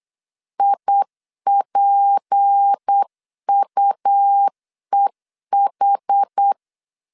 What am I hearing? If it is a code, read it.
Morse code, heard as a keyed tone, about 8 wpm, 800 Hz. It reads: IPUEH
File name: LEZIONI / 7 GF9Z / 7 GF 8wpm 3108